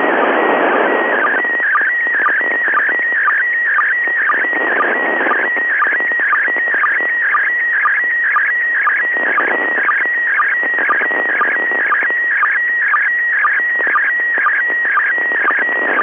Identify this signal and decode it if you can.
SSTV (slow-scan television), PD 120.
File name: dc9dd / isssstv